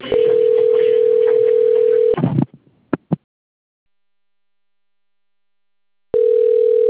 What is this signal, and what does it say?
Ringback tone (North America), 440 + 480 Hz.